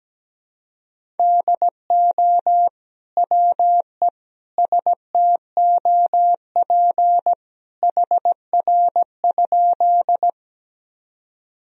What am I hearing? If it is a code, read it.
Morse code, heard as a keyed tone, about 17 wpm, 700 Hz. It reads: DO WE STOP HR?
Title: Sentences from Top 300 Words and Abbreviations 17wpm